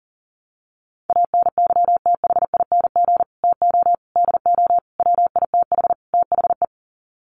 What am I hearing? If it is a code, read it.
Morse code, heard as a keyed tone, about 37 wpm, 700 Hz. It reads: ANYTHING TO DO WITH THE
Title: Sets of 5 Words Encore 40wpm